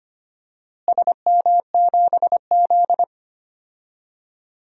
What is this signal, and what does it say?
Morse code, heard as a keyed tone, about 25 wpm, 700 Hz. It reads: SM7Z
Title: DXCC-100 Call Signs - 4 Characters - Repeated 2x - 25wpm